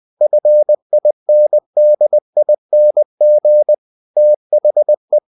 Morse code, heard as a keyed tone, about 20 wpm, 600 Hz. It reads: FINDING THE
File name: cw-books / winnie-the-pooh / winnie-20wpm-page-0016